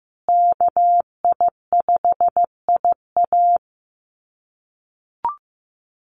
Morse code, heard as a keyed tone, about 15 wpm, 700 Hz. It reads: KI5IA
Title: USA Call Signs Repeated 3x 15wpm